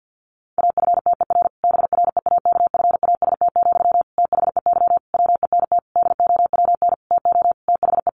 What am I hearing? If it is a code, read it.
Morse code, heard as a keyed tone, about 50 wpm, 700 Hz. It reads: AFTER BREAKFAST, THEY WENT DOWN TO THE
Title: Sentences from Top 1500 Words 50wpm